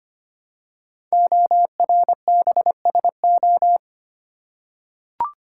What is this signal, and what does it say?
Morse code, heard as a keyed tone, about 25 wpm, 700 Hz. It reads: ORBSO
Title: Code Groups - 5 Letter:Numbers - 25wpm